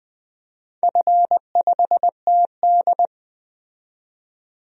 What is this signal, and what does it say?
Morse code, heard as a keyed tone, about 20 wpm, 700 Hz. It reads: F5TD
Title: DXCC-100 Call Signs - 4 Characters - 20wpm